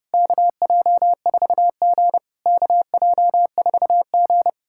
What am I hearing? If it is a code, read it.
Morse code, heard as a keyed tone, about 30 wpm, 690 Hz. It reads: KJ4G KJ4G
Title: SOTA QSO Practice 30wpm